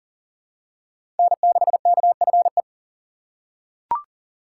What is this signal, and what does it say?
Morse code, heard as a keyed tone, about 37 wpm, 700 Hz. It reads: N6KFE
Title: USA Call Signs - 5 Characters - 6x Character Spacing - 40wpm